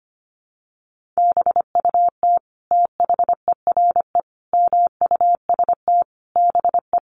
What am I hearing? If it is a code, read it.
Morse code, heard as a keyed tone, about 25 wpm, 700 Hz. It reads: BUT THERE MUST BE
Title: Sentences from Top 1500 Words 25wpm